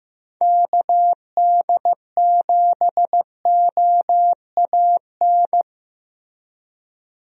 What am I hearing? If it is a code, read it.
Morse code, heard as a keyed tone, about 15 wpm, 700 Hz. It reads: KD7OAN
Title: USA Call Signs - 6 Characters - Repeated 3x - 15wpm